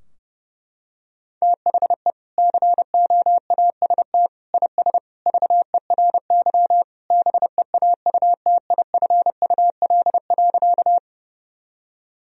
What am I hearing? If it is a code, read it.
Morse code, heard as a keyed tone, about 30 wpm, 700 Hz. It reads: THE COAST IS VERY BEAUTIFUL.
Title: Sentences from Top 1000 Words - Review Part 4 - 30wpm